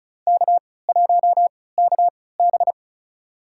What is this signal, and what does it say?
Morse code, heard as a keyed tone, about 36 wpm, spaced as for 20 wpm, 700 Hz. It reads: K1KB
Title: USA Call Signs - 4 Characters - 3x Character Spacing - 35wpm